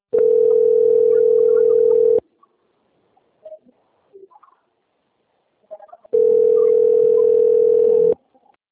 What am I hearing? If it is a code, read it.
Ringback tone (North America), 440 + 480 Hz.
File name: out- - - - - .5296